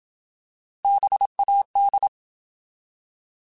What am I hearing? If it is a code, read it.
Morse code, heard as a keyed tone, about 27 wpm, 800 Hz. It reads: BAD
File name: Alphabetical / Alphabetical mp3s / Bs @25